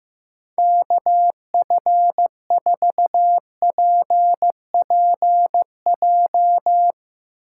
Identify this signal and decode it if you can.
Morse code, heard as a keyed tone, about 15 wpm, 700 Hz. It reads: KF4PPJ
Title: USA Call Signs - 6 Characters - 2x Character Spacing - 15wpm